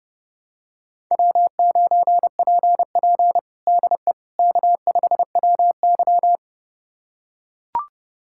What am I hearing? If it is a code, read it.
Morse code, heard as a keyed tone, about 30 wpm, 700 Hz. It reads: W9PP DE K5WY
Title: USA Callsign DE Callsign - 4 Characters - 30wpm